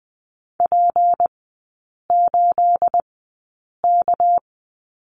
Morse code, heard as a keyed tone, about 19 wpm, spaced as for 8 wpm, 700 Hz. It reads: P8K
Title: 218 Lesson 35e - Number 0 - Three Characters 20wpm